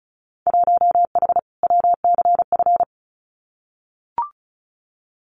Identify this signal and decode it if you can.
Morse code, heard as a keyed tone, about 36 wpm, 690 Hz. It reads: 1H WCF
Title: Winter Field Day Exchanges 35wpm